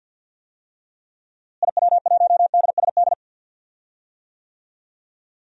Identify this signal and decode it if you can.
Morse code, heard as a keyed tone, about 53 wpm, 700 Hz. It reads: IW1DSD
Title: DXCC-100 Call Signs - 6 Characters - Repeated 2x - 50wpm